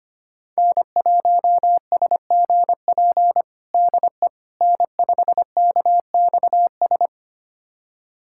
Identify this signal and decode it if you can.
Morse code, heard as a keyed tone, about 25 wpm, 700 Hz. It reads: N1SGP DE N5KXS
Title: USA Callsign DE Callsign - 5 Characters - 25wpm